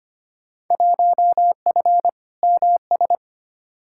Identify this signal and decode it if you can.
Morse code, heard as a keyed tone, about 25 wpm, 700 Hz. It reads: 1F MS
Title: Field Day Exchanges Encore - 25wpm